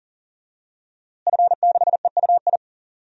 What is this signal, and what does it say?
Morse code, heard as a keyed tone, about 40 wpm, 700 Hz. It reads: F6EUI
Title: DXCC-100 Call Signs - 5 Characters - Repeated 2x - 40wpm